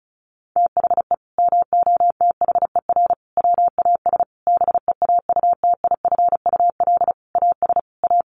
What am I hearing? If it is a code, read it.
Morse code, heard as a keyed tone, about 34 wpm, 700 Hz. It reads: THE MOTHER WAS BEAUTIFUL AS A
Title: Sentences from Top 1500 Words 35wpm